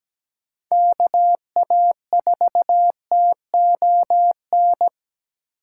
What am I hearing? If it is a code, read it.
Morse code, heard as a keyed tone, about 17 wpm, 700 Hz. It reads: KA4TON
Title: USA Call Signs - 6 Characters - 4x Character Spacing - 17wpm